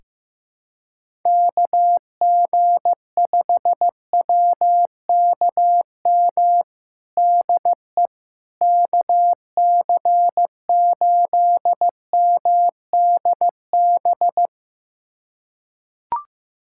Morse code, heard as a keyed tone, about 15 wpm, 700 Hz. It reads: KG5WKM DE KC8MDB
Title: USA Callsign DE Callsign 15wpm